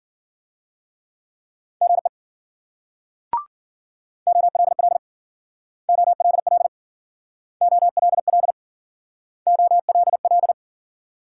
Morse code, heard as a keyed tone, about 53 wpm, 700 Hz. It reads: GE OLL OLL OLL OLL